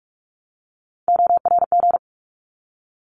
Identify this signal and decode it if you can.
Morse code, heard as a keyed tone, about 46 wpm, 700 Hz. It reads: ORG
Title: ARRL Sections 45wpm